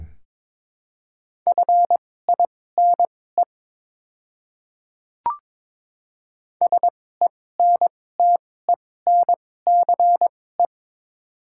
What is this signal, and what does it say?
Morse code, heard as a keyed tone, about 22 wpm, spaced as for 16 wpm, 700 Hz. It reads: FINE SENTENCE
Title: 2x Character Spacing - Top 500 Words - 22wpm